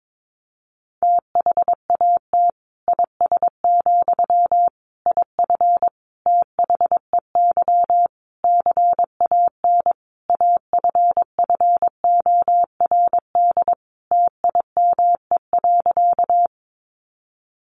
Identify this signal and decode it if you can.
Morse code, heard as a keyed tone, about 22 wpm, 700 Hz. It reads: THAT IS, IF THEY CAN AFFORD TIME.
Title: Sentences from Top 1100 Words 22wpm